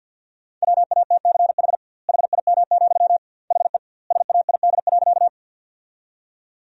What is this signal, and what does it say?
Morse code, heard as a keyed tone, about 50 wpm, 700 Hz. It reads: WATCH HIM, HE SAID.
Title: 2x Word-Spacing - Sentences from Top 300 Words - 50wpm